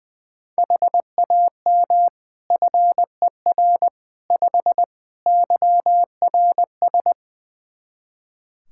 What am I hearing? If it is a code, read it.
Morse code, heard as a keyed tone, about 21 wpm, 700 Hz. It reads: HAM FER 5 YRS
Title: K1USN SST QSO Practice 20wpm